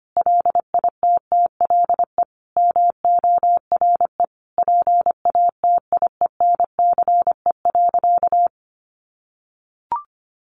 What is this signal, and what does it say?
Morse code, heard as a keyed tone, about 25 wpm, 700 Hz. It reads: LITTLE MORE PATIENCE.
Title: Sentences from Top 1300 Words 25wpm